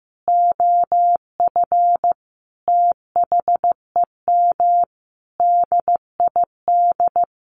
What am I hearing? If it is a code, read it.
Morse code, heard as a keyed tone, about 15 wpm, 700 Hz. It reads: OF THEM DID
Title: Sentences from Top 1500 Words 15wpm